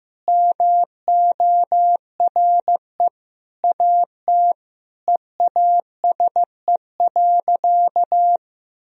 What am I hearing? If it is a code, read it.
Morse code, heard as a keyed tone, about 15 wpm, 700 Hz. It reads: MORE AT EASE.
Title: Sentences from Top 1000 Words - Review Part 1 - 15wpm